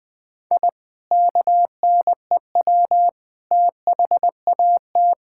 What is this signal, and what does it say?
Morse code, heard as a keyed tone, about 20 wpm, 700 Hz. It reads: I KNEW THAT
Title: Sentences with 2-Word Phrases - Spoken Beginning - 20wpm